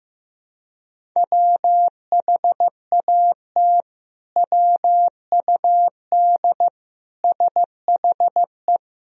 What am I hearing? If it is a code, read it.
Morse code, heard as a keyed tone, about 15 wpm, 700 Hz. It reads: WHAT WUD SHE
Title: Sentences from Top 100 Words and Abbreviations 15wpm